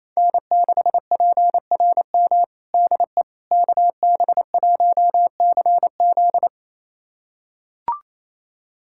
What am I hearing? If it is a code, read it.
Morse code, heard as a keyed tone, about 28 wpm, 700 Hz. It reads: N6PRM DE KB1CZ
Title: USA Callsign DE Callsign - 5 Characters - 28wpm